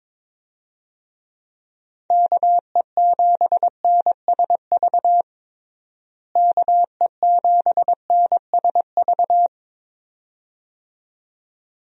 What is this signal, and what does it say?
Morse code, heard as a keyed tone, about 22 wpm, 700 Hz. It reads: KE7NSV KE7NSV
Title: USA Call Signs - 6 Characters - Repeated 2x - 22wpm